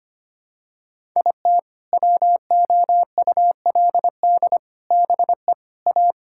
Morse code, heard as a keyed tone, about 25 wpm, 700 Hz. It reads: IT WOULD BE A
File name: practice / 25wpm / Sentences with 4-Word Phrases - 25wpm